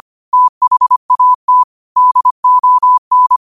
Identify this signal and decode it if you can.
Morse code, heard as a keyed tone, about 25 wpm, 1000 Hz. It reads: THAT DON
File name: 25wpm